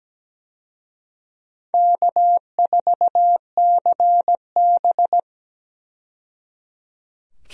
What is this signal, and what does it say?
Morse code, heard as a keyed tone, about 17 wpm, 700 Hz. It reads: K4CB
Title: USA Call Signs - 4 Characters - 17wpm